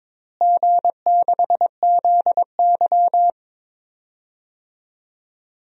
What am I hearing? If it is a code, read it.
Morse code, heard as a keyed tone, about 22 wpm, 700 Hz. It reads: G6ZY
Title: DXCC-100 Call Signs - 4 Characters - Repeated 2x - 22wpm